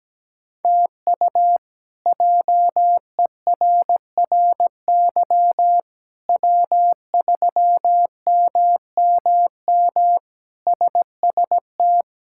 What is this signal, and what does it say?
Morse code, heard as a keyed tone, about 17 wpm, 700 Hz. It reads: TU JERRY W3MMM SST
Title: K1USN SST QSO Practice - 17wpm